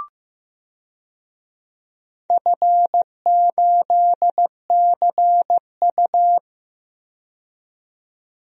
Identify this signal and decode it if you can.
Morse code, heard as a keyed tone, about 15 wpm, 700 Hz. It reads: F8CU
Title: DXCC-100 Call Signs - 4 Characters - 15wpm